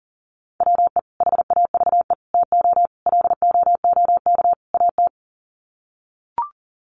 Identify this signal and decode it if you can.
Morse code, heard as a keyed tone, about 40 wpm, 700 Hz. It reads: WE HAVE TO LOOK AT
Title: Sets of 5 Words Encore2 40wpm